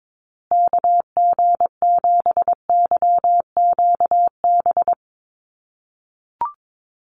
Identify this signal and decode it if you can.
Morse code, heard as a keyed tone, about 22 wpm, 700 Hz. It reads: KG7YQB
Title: USA Call Signs Repeated 2x 22wpm